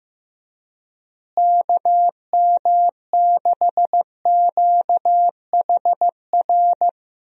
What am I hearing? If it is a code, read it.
Morse code, heard as a keyed tone, about 15 wpm, 700 Hz. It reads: KM6QHR
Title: USA Call Signs - 6 Characters - Repeated 2x - 15wpm